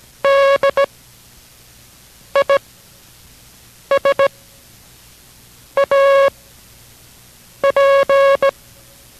Morse code, heard as a keyed tone, about 17 wpm, spaced as for 5 wpm, 550 Hz. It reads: DISAP